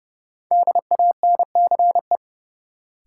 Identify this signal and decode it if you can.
Morse code, heard as a keyed tone, about 30 wpm, 700 Hz. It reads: DANCE
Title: Sentences from Top 700 Words - New Word Drill - 30wpm